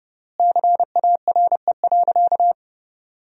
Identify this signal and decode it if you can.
Morse code, heard as a keyed tone, about 30 wpm, 700 Hz. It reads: CARE.
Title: 3x Word-Spacing - Sentences from Top 300 Words - 30wpm